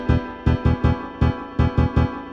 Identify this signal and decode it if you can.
There is no code.